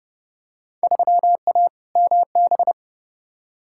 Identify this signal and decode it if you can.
Morse code, heard as a keyed tone, about 32 wpm, 700 Hz. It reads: 3A MB
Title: Field Day Exchanges Encore - 30wpm